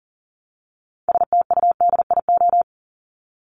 Morse code, heard as a keyed tone, about 40 wpm, 700 Hz. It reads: STUDIO
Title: Top 1001-1250 Words 40wpm